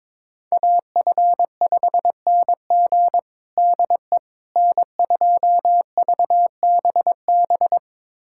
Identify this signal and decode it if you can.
Morse code, heard as a keyed tone, about 22 wpm, 700 Hz. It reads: AF5NG DE N2VBB
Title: USA Callsign DE Callsign - 5 Characters - 22wpm